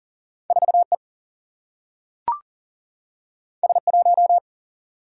Morse code, heard as a keyed tone, about 40 wpm, 700 Hz. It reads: 4 E S 1
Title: Single-Letter Single-Number 40wpm